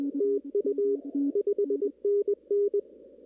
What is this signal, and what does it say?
Morse code, heard as a keyed tone, about 20 wpm, 410 Hz. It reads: TU 5NN
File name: CW / kt5j wpx cw-op-k5pi